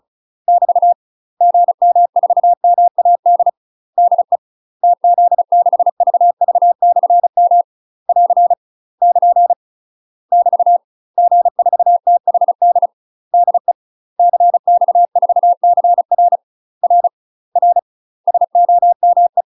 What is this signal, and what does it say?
Morse code, heard as a keyed tone, about 35 wpm, 700 Hz. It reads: <BT> GM4MAD DE TZ6VV/M <AR> <KN> <BT> G4THD DE CX4CR R R SOME
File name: UK MASTER QSOs TRAINING/35WPM